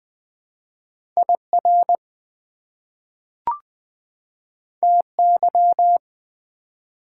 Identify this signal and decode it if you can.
Morse code, heard as a keyed tone, about 20 wpm, 700 Hz. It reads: IR TY
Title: Prefixes and Suffixes 20wpm